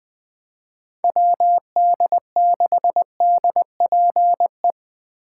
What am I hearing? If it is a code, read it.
Morse code, heard as a keyed tone, about 24 wpm, 700 Hz. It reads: WD6DPE
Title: Speed Racing - USA 2x3 Call Signs 20wpm